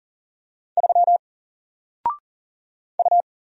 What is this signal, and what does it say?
Morse code, heard as a keyed tone, about 38 wpm, 700 Hz. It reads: 3 U
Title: Single Letter-Number 40wpm